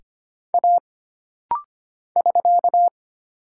Morse code, heard as a keyed tone, about 25 wpm, 700 Hz. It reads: A <SK>
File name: Single Character (Letter:Number:Punctuation:Prosign) - Rapid-Fire - 25wpm